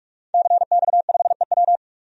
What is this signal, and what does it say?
Morse code, heard as a keyed tone, about 44 wpm, 700 Hz. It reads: CX5EW
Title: DXCC-100 Call Signs - 5 Characters - 45wpm